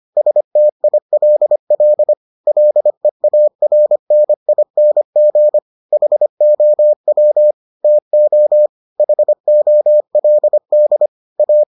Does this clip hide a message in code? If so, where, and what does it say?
Yes: Morse code from 0.2 to 11.6 s, heard as a keyed tone, about 25 wpm, 600 Hz. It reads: STILL LEARNING HOW TO HOLD A